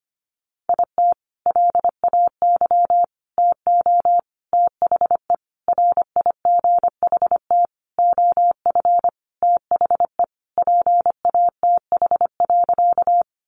Morse code, heard as a keyed tone, about 25 wpm, 700 Hz. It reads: IT LAY TO THE RIGHT OF THE PATH.
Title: Sentences from Top 1400 Words 25wpm